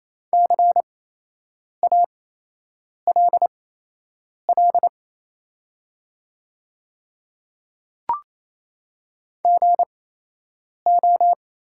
Morse code, heard as a keyed tone, about 27 wpm, spaced as for 8 wpm, 700 Hz. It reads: CALL GO
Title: 8x Character Spacing - Top 100 Words - 28wpm